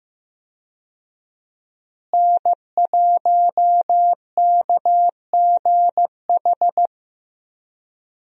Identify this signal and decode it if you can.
Morse code, heard as a keyed tone, about 15 wpm, 700 Hz. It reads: N1KGH
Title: USA Call Signs - 5 Characters - Repeated 2x - 15wpm